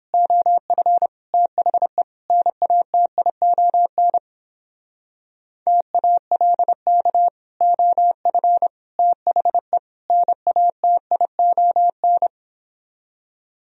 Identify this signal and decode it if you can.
Morse code, heard as a keyed tone, about 29 wpm, 700 Hz. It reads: OF THE NATION TALK OF THE NATION